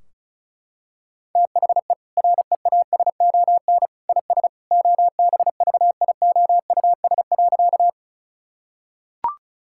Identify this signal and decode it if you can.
Morse code, heard as a keyed tone, about 34 wpm, 700 Hz. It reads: THE REASON IS OBVIOUS.
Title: Sentences from Top 1600 Words 35wpm